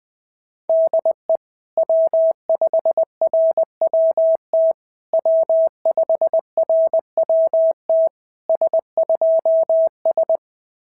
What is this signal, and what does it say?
Morse code, heard as a keyed tone, about 19 wpm, 650 Hz. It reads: DE W5RWT W5RWT S2S
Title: SOTA QSO Practice 20wpm